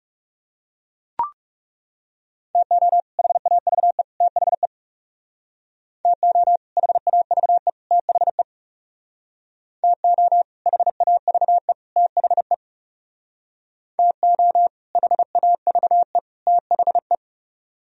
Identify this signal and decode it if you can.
Morse code, heard as a keyed tone, about 37 wpm, 700 Hz. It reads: TO HAVE THE TO HAVE THE TO HAVE THE TO HAVE THE
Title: Speed-Racing - Sets of 3 Words 30wpm